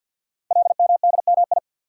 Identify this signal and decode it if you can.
Morse code, heard as a keyed tone, about 48 wpm, 700 Hz. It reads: LMDMI
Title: Code Groups - 5 Letters - 50wpm